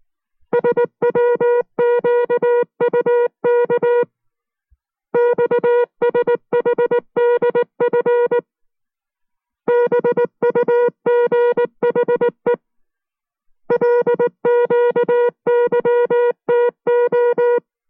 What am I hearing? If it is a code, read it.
Morse code, heard as a keyed tone, about 19 wpm, 480 Hz. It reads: SWQUK XSHDF BUGHE LQYTO